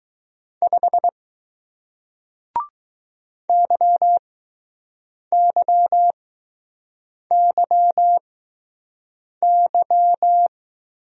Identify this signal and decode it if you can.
Morse code, heard as a keyed tone, about 20 wpm, spaced as for 6 wpm, 700 Hz. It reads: 5 YYYY